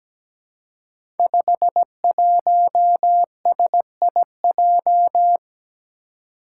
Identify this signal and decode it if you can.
Morse code, heard as a keyed tone, about 17 wpm, 700 Hz. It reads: 51SIJ
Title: Five Challenging Letter-Number Combinations 17wpm